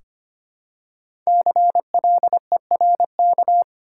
Morse code, heard as a keyed tone, about 25 wpm, 700 Hz. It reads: CLERK
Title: Five Letter Words - 25wpm